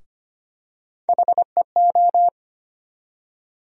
Morse code, heard as a keyed tone, about 25 wpm, 700 Hz. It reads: HEO
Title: USA Call Sign Postfixes - 3 Letters 25wpm